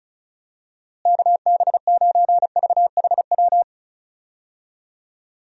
Morse code, heard as a keyed tone, about 34 wpm, 700 Hz. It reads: KB9VHW
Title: USA Call Signs - 6 Characters - Repeated 2x - 35wpm